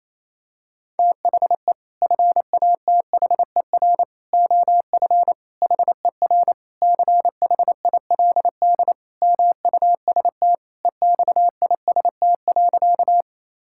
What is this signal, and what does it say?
Morse code, heard as a keyed tone, about 28 wpm, 700 Hz. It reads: THE FATHER OF HER CHILD MUST EXIST.
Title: Sentences from Top 1000 Words - Review Part 5 - 28wpm